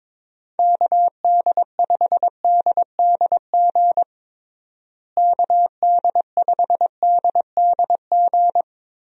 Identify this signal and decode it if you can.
Morse code, heard as a keyed tone, about 22 wpm, 700 Hz. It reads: KD5DDG KD5DDG
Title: USA Call Signs - 6 Characters - Repeated 2x - 22wpm